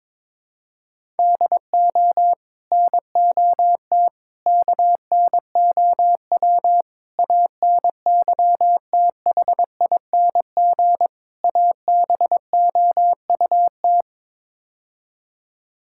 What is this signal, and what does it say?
Morse code, heard as a keyed tone, about 21 wpm, 700 Hz. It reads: DO NOT KNOW ANYTHING ABOUT
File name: practice / 22wpm / Build A Phrase - 5-Word Phrases - 22wpm